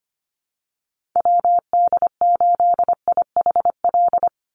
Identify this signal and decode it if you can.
Morse code, heard as a keyed tone, about 25 wpm, 700 Hz. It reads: WD8IHL
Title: USA Call Signs 25wpm